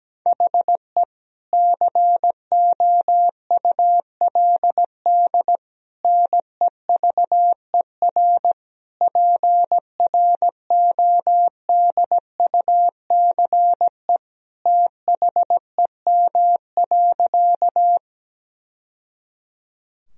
Morse code, heard as a keyed tone, about 17 wpm, 700 Hz. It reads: HE COULD NEVER PRODUCE THEM.
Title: Sentences from Top 1000 Words - Review Part 3 - 17wpm